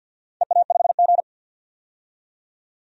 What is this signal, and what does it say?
Morse code, heard as a keyed tone, about 49 wpm, 700 Hz. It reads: EA5G
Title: DXCC-100 Call Signs - 4 Characters - 50wpm